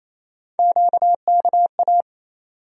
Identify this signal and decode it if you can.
Morse code, heard as a keyed tone, about 28 wpm, 700 Hz. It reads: QKA
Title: USA Call Sign Postfixes - 3 Letters 28wpm